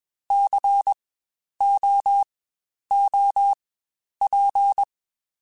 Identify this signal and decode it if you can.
Morse code, heard as a keyed tone, about 21 wpm, spaced as for 10 wpm, 800 Hz. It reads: COOP
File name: Spaced Alphas / Spaced Alphas mp3s / Cs sp @20